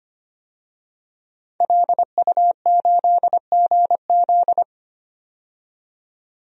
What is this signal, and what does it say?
Morse code, heard as a keyed tone, about 25 wpm, 700 Hz. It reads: LU8GZ
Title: DXCC-100 Call Signs - 5 Characters - 25wpm